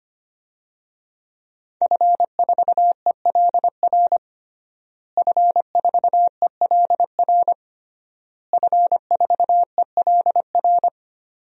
Morse code, heard as a keyed tone, about 25 wpm, 700 Hz. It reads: F4ELR F4ELR F4ELR
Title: DXCC-100 Call Signs - 5 Characters - Repeated 3x - 25wpm